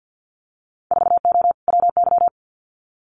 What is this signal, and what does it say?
Morse code, heard as a keyed tone, about 50 wpm, 700 Hz. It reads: 4O WY
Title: Winter Field Day Exchanges Encore 50wpm